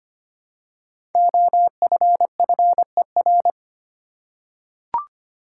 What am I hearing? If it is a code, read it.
Morse code, heard as a keyed tone, about 25 wpm, 700 Hz. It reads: OFFER
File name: 5 Letter Words - First 3 Letters Given - 25wpm